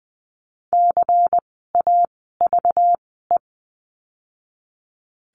Morse code, heard as a keyed tone, about 20 wpm, spaced as for 14 wpm, 700 Hz. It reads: CAVE
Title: 192 Lesson 31f - Letter v - Words 20wpm